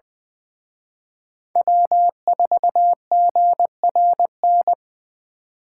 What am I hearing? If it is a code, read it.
Morse code, heard as a keyed tone, about 24 wpm, 700 Hz. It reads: W4GRN
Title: Speed Racing - USA Call Signs 20wpm